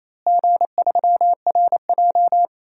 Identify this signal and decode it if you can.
Morse code, heard as a keyed tone, about 28 wpm, 700 Hz. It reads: G3RJ
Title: DXCC-100 Call Signs - 4 Characters - Repeated 3x - 28wpm